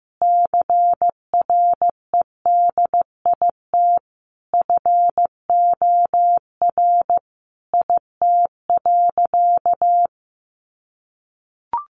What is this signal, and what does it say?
Morse code, heard as a keyed tone, about 15 wpm, 700 Hz. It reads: CREDIT FOR IT.
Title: Sentences from Top 1000 Words 15wpm